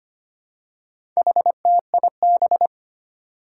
Morse code, heard as a keyed tone, about 25 wpm, 700 Hz. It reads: HTIB
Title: Four Challenging Letter-Number Combinations 25wpm